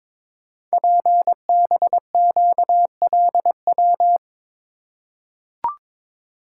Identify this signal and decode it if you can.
Morse code, heard as a keyed tone, about 22 wpm, 700 Hz. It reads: PBQLW
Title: Code Groups - 5 Letters - 22wpm